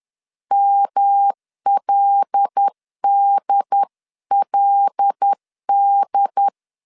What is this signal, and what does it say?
Morse code, heard as a keyed tone, about 10 wpm, 800 Hz. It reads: MLDLD